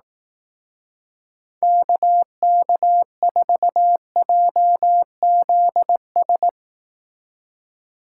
Morse code, heard as a keyed tone, about 18 wpm, 700 Hz. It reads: KK4JZS
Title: Speed Racing - USA 2x3 Call Signs 15wpm